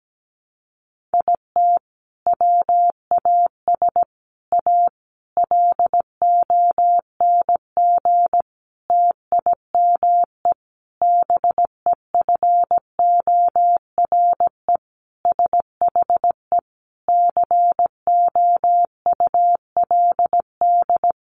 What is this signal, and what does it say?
Morse code, heard as a keyed tone, about 17 wpm, 700 Hz. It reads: IT WAS A LONG TIME BEFORE SHE COULD
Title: Sentences from Top 500 Words 17wpm